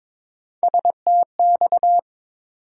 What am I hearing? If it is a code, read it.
Morse code, heard as a keyed tone, about 23 wpm, 700 Hz. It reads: STX
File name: practice / 22wpm / ARRL Sections v2 - 22wpm